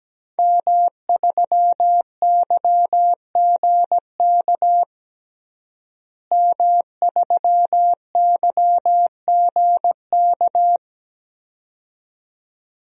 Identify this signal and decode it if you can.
Morse code, heard as a keyed tone, about 17 wpm, 700 Hz. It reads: M3YGK M3YGK
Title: DXCC-100 Call Signs - 5 Characters - Repeated 2x - 17wpm